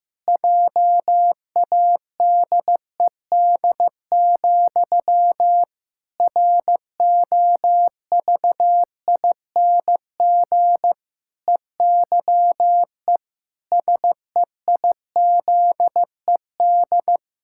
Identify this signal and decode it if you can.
Morse code, heard as a keyed tone, about 15 wpm, 700 Hz. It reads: JADED, ROVING EYE SEIZED
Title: Pangram Challenge v2 - 15wpm